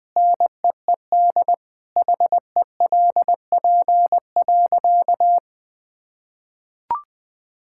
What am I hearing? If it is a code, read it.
Morse code, heard as a keyed tone, about 21 wpm, 700 Hz. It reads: NEED HELP.
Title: Sentences with 8-Word Phrases - 20wpm